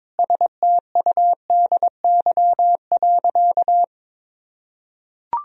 Morse code, heard as a keyed tone, about 22 wpm, 700 Hz. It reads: STUDY.
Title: Sentences from Top 1000 Words - Review Part 1 - 22wpm